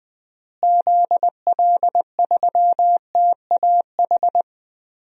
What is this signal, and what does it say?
Morse code, heard as a keyed tone, about 20 wpm, 700 Hz. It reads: ZL3TAH
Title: DXCC-100 Call Signs - 6 Characters - Repeated 3x - 20wpm